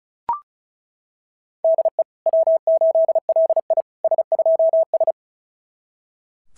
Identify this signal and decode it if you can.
Morse code, heard as a keyed tone, about 36 wpm, 690 Hz. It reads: DE W8LI S2S
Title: SOTA QSO Practice 35wpm